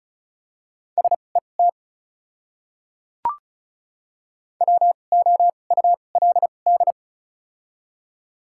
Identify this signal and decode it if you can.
Morse code, heard as a keyed tone, about 34 wpm, spaced as for 25 wpm, 700 Hz. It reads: SET WOULD
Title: 2x Character Spacing - Top 300 Words - 35wpm